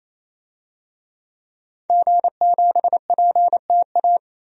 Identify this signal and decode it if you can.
Morse code, heard as a keyed tone, about 28 wpm, 700 Hz. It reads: G7PTA
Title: DXCC-100 Call Signs - 5 Characters - 28wpm